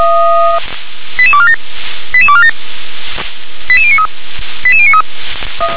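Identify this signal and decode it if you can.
Five-tone selective call (ZVEI), heard as quick runs of five tones, tones of about 70 ms. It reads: 88247 88247 88093 88093
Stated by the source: Five-tone ZVEI selective call received by SDR.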